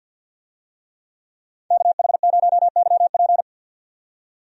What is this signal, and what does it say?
Morse code, heard as a keyed tone, about 51 wpm, 700 Hz. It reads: KH0YP
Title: USA Call Signs - 5 Characters - 50wpm